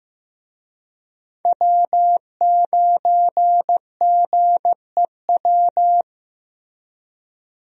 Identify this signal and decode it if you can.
Morse code, heard as a keyed tone, about 15 wpm, 700 Hz. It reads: W9GEW
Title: USA Call Signs - 5 Characters - Repeated 2x - 15wpm